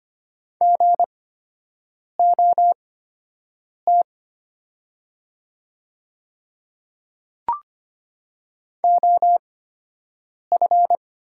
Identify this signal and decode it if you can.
Morse code, heard as a keyed tone, about 25 wpm, spaced as for 7 wpm, 700 Hz. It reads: GOT OF
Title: 8x Character Spacing - Top 300 Words - 25wpm